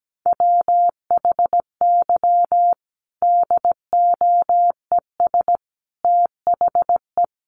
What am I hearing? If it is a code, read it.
Morse code, heard as a keyed tone, about 17 wpm, 700 Hz. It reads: WHY DOES THE
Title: Sentences from Top 500 Words 17wpm